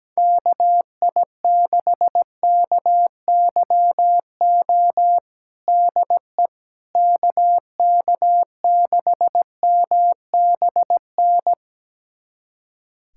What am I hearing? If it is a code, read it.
Morse code, heard as a keyed tone, about 17 wpm, 700 Hz. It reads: KI6KYO DE KK6MBN
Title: USA Callsign DE Callsign - 6 Characters - 17wpm